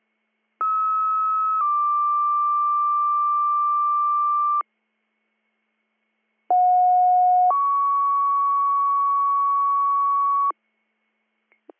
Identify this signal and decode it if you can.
Two-tone page, heard as a one-second tone, then a three-second tone, about 1290 then 1180 Hz, about 730 then 1120 Hz. It is heard from 0.6 to 10.5 s.